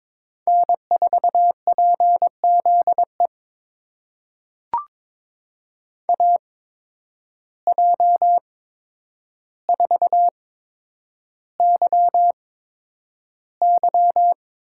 Morse code, heard as a keyed tone, about 22 wpm, spaced as for 9 wpm, 700 Hz. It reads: N4PZE AJ4YY
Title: USA Call Signs - 8x Character Spacing - 22wpm